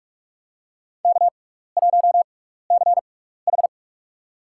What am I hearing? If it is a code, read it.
Morse code, heard as a keyed tone, about 44 wpm, spaced as for 15 wpm, 700 Hz. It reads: K1CH
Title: USA Call Signs - 6x Character Spacing - 45wpm